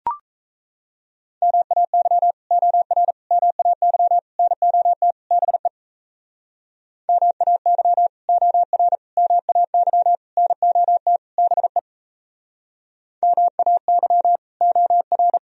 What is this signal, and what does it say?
Morse code, heard as a keyed tone, about 38 wpm, 700 Hz. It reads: MAY OR MAY NOT BE MAY OR MAY NOT BE MAY OR